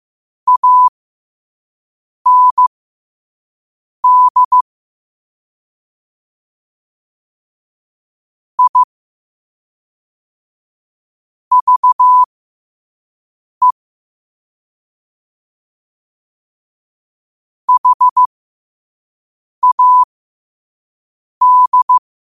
Morse code, heard as a keyed tone, about 15 wpm, spaced as for 5 wpm, 1000 Hz. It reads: AND I VE HAD
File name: qotdfiles / 5wpm